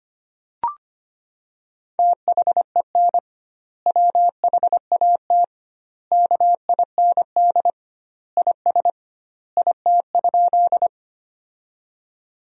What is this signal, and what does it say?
Morse code, heard as a keyed tone, about 25 wpm, 700 Hz. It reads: THEN WHAT KIND IS IT?
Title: 2x Word-Spacing - Sentences from Top 200 Words - 25wpm